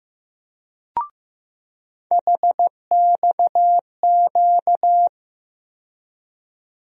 Morse code, heard as a keyed tone, about 15 wpm, 700 Hz. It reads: HXQ
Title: USA Call Sign Postfixes - 3 Letters 15wpm